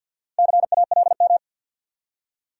Morse code, heard as a keyed tone, about 49 wpm, 700 Hz. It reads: CALM
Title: Four Letter Words - 50wpm